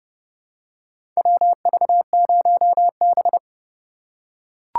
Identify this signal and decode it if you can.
Morse code, heard as a keyed tone, about 29 wpm, 700 Hz. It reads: WV0B
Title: USA Call Signs - 4 Characters - 6x Character Spacing - 30wpm